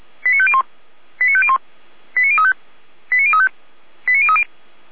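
Five-tone selective call (ZVEI), heard as quick runs of five tones, tones of about 70 ms. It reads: 89501 89501 89035 89035 89030
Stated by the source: ZVEI five-tone call 89514 received by SDR.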